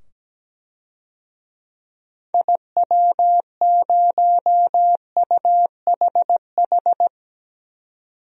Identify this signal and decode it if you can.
Morse code, heard as a keyed tone, about 17 wpm, 700 Hz. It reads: IW0UHH
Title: DXCC-100 Call Signs - 6 Characters - 17wpm